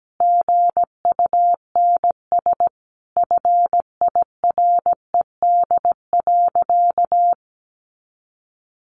Morse code, heard as a keyed tone, about 17 wpm, 700 Hz. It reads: GUNS FIRED.
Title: A Sound Of Thunder 17wpm